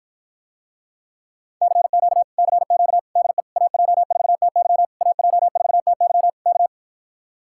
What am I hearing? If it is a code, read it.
Morse code, heard as a keyed tone, about 53 wpm, 700 Hz. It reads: CQ CQ DE AJ4TY AJ4TY K